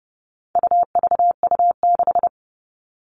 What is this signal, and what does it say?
Morse code, heard as a keyed tone, about 30 wpm, 700 Hz. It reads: UVU6
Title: All Cut Numbers - 4-Digit 30wpm